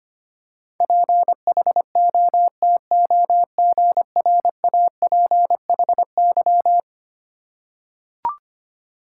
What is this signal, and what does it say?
Morse code, heard as a keyed tone, about 25 wpm, 700 Hz. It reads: PHOTOGRAPHY
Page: morse-code-ninja-practice-25wpm